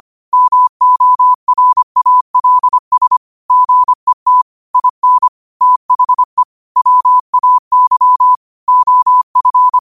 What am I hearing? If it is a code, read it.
Morse code, heard as a keyed tone, about 25 wpm, 1000 Hz. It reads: MORALS GET IN THE WAY OF
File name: qotdfiles / 25wpm